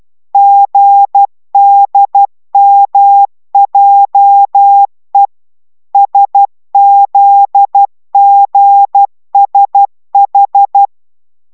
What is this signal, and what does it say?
Morse code, heard as a keyed tone, about 12 wpm, 800 Hz. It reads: GDMJE SZGSH